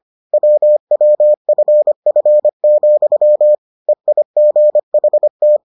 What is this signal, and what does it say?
Morse code, heard as a keyed tone, about 25 wpm, 600 Hz. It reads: WWFF, EIGHT